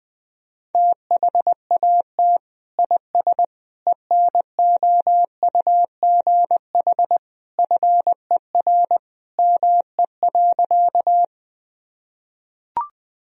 Morse code, heard as a keyed tone, about 19 wpm, 700 Hz. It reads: THAT IS ENOUGH FER ME.
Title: Sentences from Top 400 Words and Abbreviations 20wpm